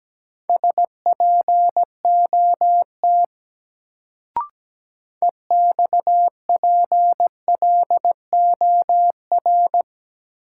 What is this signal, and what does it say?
Morse code, heard as a keyed tone, about 17 wpm, 700 Hz. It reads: SPOT EXPLOR
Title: Sentences from Top 1100 Words - New Word Drill Alt - 17wpm